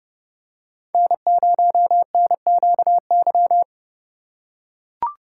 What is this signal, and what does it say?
Morse code, heard as a keyed tone, about 30 wpm, 700 Hz. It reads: N0NQY
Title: USA Call Signs - 2x Character Spacing - 30wpm